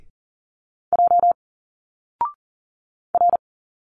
Morse code, heard as a keyed tone, about 40 wpm, 700 Hz. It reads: J R
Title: Single Letters 40wpm